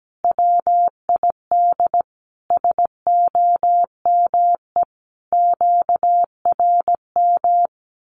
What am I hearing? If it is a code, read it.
Morse code, heard as a keyed tone, about 17 wpm, 700 Hz. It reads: WID SOME QRM
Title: Common QSO Phrases 17wpm